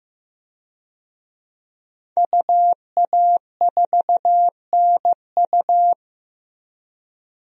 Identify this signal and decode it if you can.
Morse code, heard as a keyed tone, about 15 wpm, 700 Hz. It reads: UA4NU
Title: DXCC-100 Call Signs - 5 Characters - Repeated 3x - 15wpm